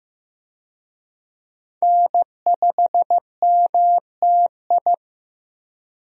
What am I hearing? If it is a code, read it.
Morse code, heard as a keyed tone, about 15 wpm, 700 Hz. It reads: N5MTI
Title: USA Call Signs - 5 Characters - Repeated 3x - 15wpm